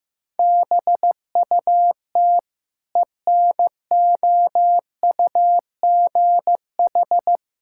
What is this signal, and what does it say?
Morse code, heard as a keyed tone, about 15 wpm, 700 Hz. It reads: BUT ENOUGH
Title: Sentences from Top 1000 Words - Review Part 6 - 15wpm